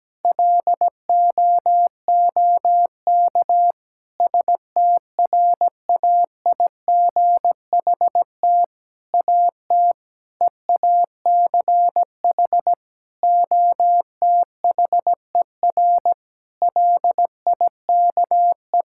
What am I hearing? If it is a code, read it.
Morse code, heard as a keyed tone, about 17 wpm, 700 Hz. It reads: LOOK STRAIGHT AT EACH OTHER LIKE
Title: Sentences from Top 1000 Words - Review Part 6 - 17wpm